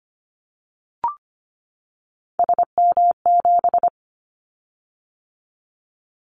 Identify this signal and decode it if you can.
Morse code, heard as a keyed tone, about 25 wpm, 700 Hz. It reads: SM7
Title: Callsign Prefixes for Top 25 DXCC - 25wpm